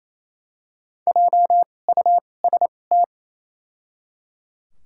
Morse code, heard as a keyed tone, about 28 wpm, 700 Hz. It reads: J U S T
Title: 2x Character Spacing - Top 200 Words - 28wpm